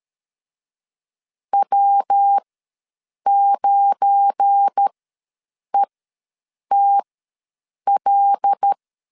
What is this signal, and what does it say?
Morse code, heard as a keyed tone, about 13 wpm, spaced as for 7 wpm, 800 Hz. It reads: W9ETL
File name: LEZIONI / 9 28 ws 12wpm 3354